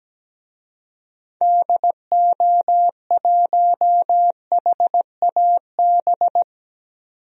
Morse code, heard as a keyed tone, about 17 wpm, 700 Hz. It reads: DO1HAB
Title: DXCC-100 Call Signs - 6 Characters - 17wpm